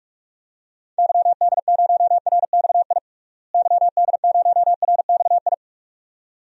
Morse code, heard as a keyed tone, about 46 wpm, 700 Hz. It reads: YD0RXI YD0RXI
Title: DXCC-100 Call Signs - 6 Characters - Repeated 2x - 45wpm